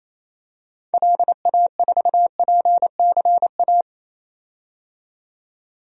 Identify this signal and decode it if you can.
Morse code, heard as a keyed tone, about 28 wpm, 700 Hz. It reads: LA4PCA
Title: DXCC-100 Call Signs - 6 Characters - 28wpm